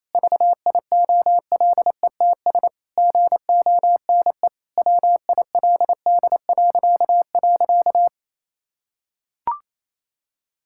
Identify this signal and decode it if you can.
Morse code, heard as a keyed tone, about 28 wpm, 700 Hz. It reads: VIOLETS GONE WILD..
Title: Pangram Challenge v2 - 28wpm